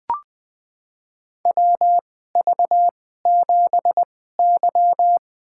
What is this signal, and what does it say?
Morse code, heard as a keyed tone, about 20 wpm, spaced as for 14 wpm, 700 Hz. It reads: WV7Y
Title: USA Call Signs - 4 Characters - 2x Character Spacing - 20wpm